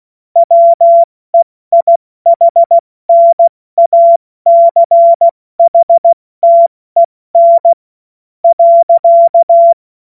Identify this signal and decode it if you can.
Morse code, heard as a keyed tone, about 16 wpm, spaced as for 13 wpm, 680 Hz. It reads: WEIHNACHTEN.
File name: Morsekurs / Witze / Witz 16. Mai 2023 70 F80